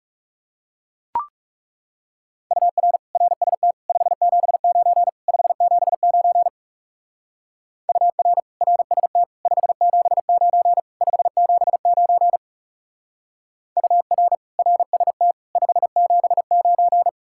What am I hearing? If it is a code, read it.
Morse code, heard as a keyed tone, about 38 wpm, 700 Hz. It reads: UR RST 579 579 UR RST 579 579 UR RST 579